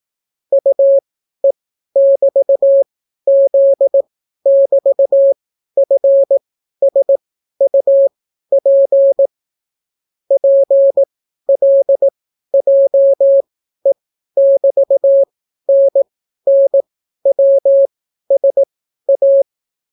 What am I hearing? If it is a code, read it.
Morse code, heard as a keyed tone, about 18 wpm, spaced as for 12 wpm, 550 Hz. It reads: UE<BT>Z<BT>FSUP PLJE<BT>NNWSA